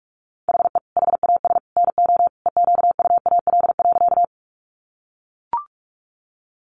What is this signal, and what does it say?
Morse code, heard as a keyed tone, about 44 wpm, 700 Hz. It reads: HE HAS NO EQUAL.
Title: Sentences from Top 1400 Words 45wpm